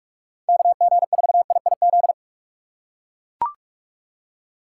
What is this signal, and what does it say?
Morse code, heard as a keyed tone, about 42 wpm, 700 Hz. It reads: KG4IIZ
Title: USA Call Signs - 6 Characters - 2x Character Spacing - 45wpm